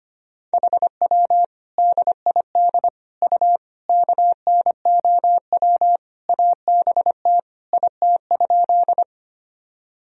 Morse code, heard as a keyed tone, about 25 wpm, 700 Hz. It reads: HW DID U KNOW ABT IT?
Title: Sentences from Top 100 Words and Abbreviations 25wpm